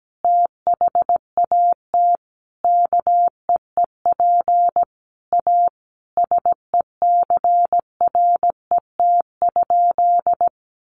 Morse code, heard as a keyed tone, about 17 wpm, 700 Hz. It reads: THAT KEEP A SECRET?
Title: Sentences from Top 1200 Words 17wpm